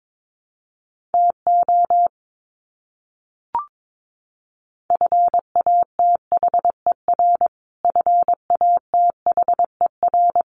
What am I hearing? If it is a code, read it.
Morse code, heard as a keyed tone, about 22 wpm, 700 Hz. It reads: TO FATHER FATHER
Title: Top 300 Words Repeated 22wpm